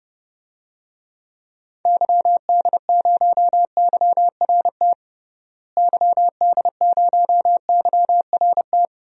Morse code, heard as a keyed tone, about 30 wpm, 700 Hz. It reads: YD0YRT YD0YRT
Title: DXCC-100 Call Signs - 6 Characters - Repeated 2x - 30wpm